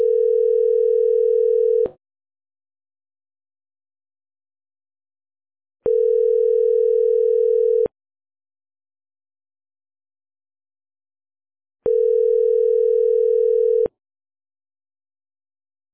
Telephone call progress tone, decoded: ringback tone